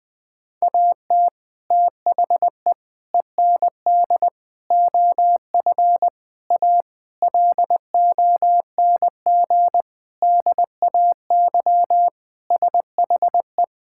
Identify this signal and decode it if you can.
Morse code, heard as a keyed tone, about 20 wpm, 700 Hz. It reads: AT THE END OF A LONG DAY SHE
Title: Sentences with 4-Word Phrases - 20wpm